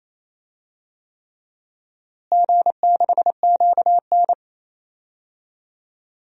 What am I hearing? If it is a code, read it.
Morse code, heard as a keyed tone, about 27 wpm, 700 Hz. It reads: G6QN
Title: DXCC-100 Call Signs - 4 Characters - Repeated 3x - 28wpm